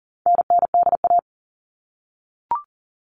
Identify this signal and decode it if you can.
Morse code, heard as a keyed tone, about 40 wpm, 700 Hz. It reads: NNDA
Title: All Cut Numbers - 4-Digit 40wpm